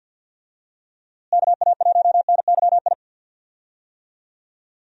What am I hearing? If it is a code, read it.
Morse code, heard as a keyed tone, about 46 wpm, 700 Hz. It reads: KA1NYI
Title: USA Call Signs - 6 Characters - Repeated 2x - 50wpm